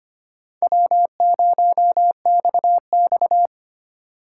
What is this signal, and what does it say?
Morse code, heard as a keyed tone, about 32 wpm, 700 Hz. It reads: W0XX